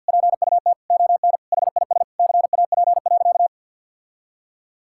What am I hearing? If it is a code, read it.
Morse code, heard as a keyed tone, about 50 wpm, 700 Hz. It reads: PUT ON HIS CAP.
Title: Sentences from Top 1600 Words 50wpm